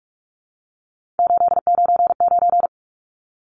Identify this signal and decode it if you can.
Morse code, heard as a keyed tone, about 44 wpm, 700 Hz. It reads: 899
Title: Numbers - 3-Digit 45wpm